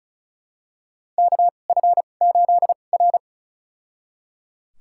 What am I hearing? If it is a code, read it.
Morse code, heard as a keyed tone, about 36 wpm, spaced as for 25 wpm, 700 Hz. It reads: KF8R
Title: USA Call Signs - 4 Characters - 2x Character Spacing - 35wpm